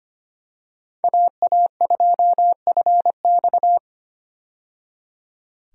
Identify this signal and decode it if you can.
Morse code, heard as a keyed tone, about 25 wpm, 700 Hz. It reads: AA2FX
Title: USA Call Signs - 5 Characters - 25wpm